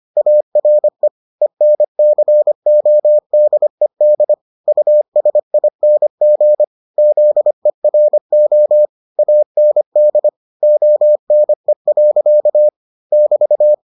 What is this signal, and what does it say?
Morse code, heard as a keyed tone, about 25 wpm, 600 Hz. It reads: ARE ENCODED USING ZERO AND ONE. <BT>